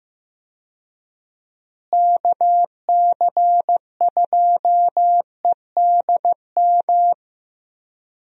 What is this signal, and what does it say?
Morse code, heard as a keyed tone, about 15 wpm, 700 Hz. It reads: KC2EDM
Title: USA Call Signs - 6 Characters - Repeated 2x - 15wpm